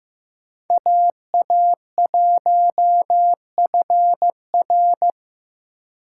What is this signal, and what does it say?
Morse code, heard as a keyed tone, about 15 wpm, 700 Hz. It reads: AA1FR
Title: USA Call Signs - 5 Characters - 4x Character Spacing - 15wpm